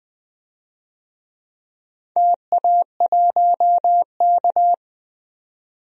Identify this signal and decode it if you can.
Morse code, heard as a keyed tone, about 20 wpm, 700 Hz. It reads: TA1K
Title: DXCC-100 Call Signs - 4 Characters - 20wpm